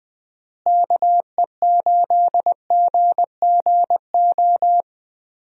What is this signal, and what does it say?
Morse code, heard as a keyed tone, about 20 wpm, 700 Hz. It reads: KE8GGO
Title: USA Call Signs - 6 Characters - Repeated 2x - 20wpm